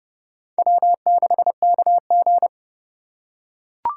Morse code, heard as a keyed tone, about 28 wpm, 700 Hz. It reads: W6KG